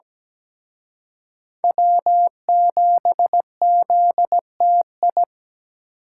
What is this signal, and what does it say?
Morse code, heard as a keyed tone, about 20 wpm, 700 Hz. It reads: W7ZTI
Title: Speed Racing - USA 1x3 Call Signs 17wpm